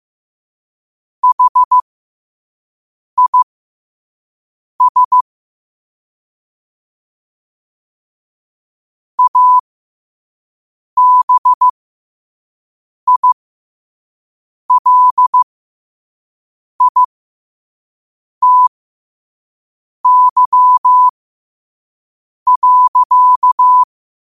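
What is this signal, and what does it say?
Morse code, heard as a keyed tone, about 15 wpm, spaced as for 5 wpm, 1000 Hz. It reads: HIS ABILITY.
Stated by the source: Morse code quote of the day at 5 wpm.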